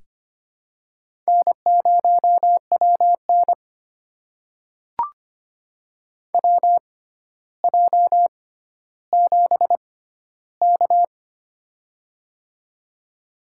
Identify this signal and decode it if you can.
Morse code, heard as a keyed tone, about 25 wpm, spaced as for 13 wpm, 700 Hz. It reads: N0WN WJ7K
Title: USA Call Signs - 4 Characters - 6x Character Spacing - 25wpm